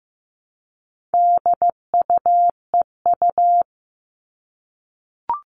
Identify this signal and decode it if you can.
Morse code, heard as a keyed tone, about 15 wpm, 700 Hz. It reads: DUEU
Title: All Cut Numbers - 4-Digit 15wpm